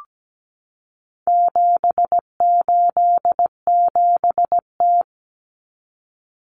Morse code, heard as a keyed tone, about 17 wpm, 700 Hz. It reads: 787T
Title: Common Cut Numbers - 4-Digit 17wpm